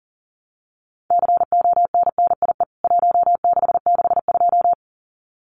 Morse code, heard as a keyed tone, about 40 wpm, 700 Hz. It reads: CONNIE 1662
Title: CWT Exchange 40wpm